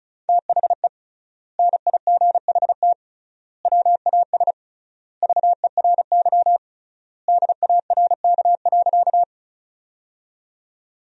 Morse code, heard as a keyed tone, about 36 wpm, 700 Hz. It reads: THE NIGHT WAS VERY DARK.
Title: 3x Word-Spacing - Sentences from Top 500 Words - 35wpm